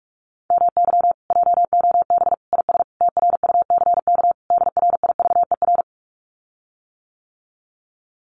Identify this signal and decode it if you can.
Morse code, heard as a keyed tone, about 46 wpm, 700 Hz. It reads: MY JOB IS TRUCK DRIVER
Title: Common QSO Phrases Encore 45wpm